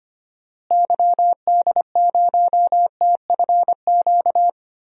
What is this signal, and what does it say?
Morse code, heard as a keyed tone, about 25 wpm, 700 Hz. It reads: YD0TFQ
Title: DXCC-100 Call Signs - 6 Characters - Repeated 3x - 25wpm